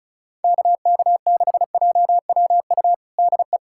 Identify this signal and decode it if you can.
Morse code, heard as a keyed tone, about 36 wpm, 700 Hz. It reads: KK6JWU DE
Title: USA Callsign DE Callsign 35wpm